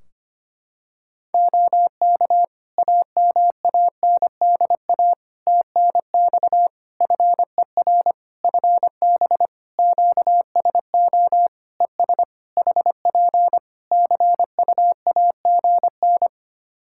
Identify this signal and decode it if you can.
Morse code, heard as a keyed tone, about 25 wpm, 700 Hz. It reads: OK AMANDA TNX FER FB QSO ES HP CUAGN
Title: Practice QSO - Exchange 4 - Ending - 25wpm